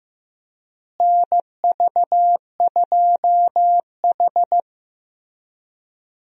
Morse code, heard as a keyed tone, about 15 wpm, 700 Hz. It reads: NV2H
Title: USA Call Signs - 4 Characters - Repeated 2x - 15wpm